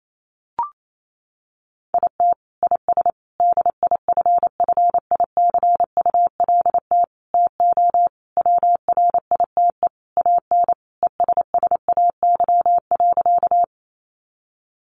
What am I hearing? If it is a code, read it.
Morse code, heard as a keyed tone, about 28 wpm, 700 Hz. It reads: IT IS DIFFICULT TO WRITE AN ESSAY.
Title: Sentences from Top 1500 Words 28wpm